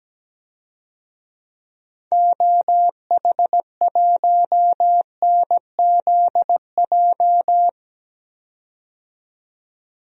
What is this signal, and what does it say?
Morse code, heard as a keyed tone, about 17 wpm, 700 Hz. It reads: OH1NZJ
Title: DXCC-100 Call Signs - 6 Characters - 17wpm